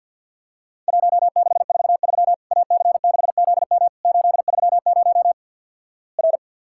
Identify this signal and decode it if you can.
Morse code, heard as a keyed tone, about 50 wpm, 700 Hz. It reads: 1643 AC6ZM 830 R
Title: Worked All Europe DX Contest - QTC Exchange with Repeats - 50wpm